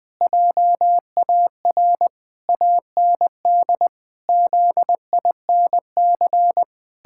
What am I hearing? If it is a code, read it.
Morse code, heard as a keyed tone, about 21 wpm, 700 Hz. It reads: JAR AND ZINC
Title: Pangram Challenge v2 - 20wpm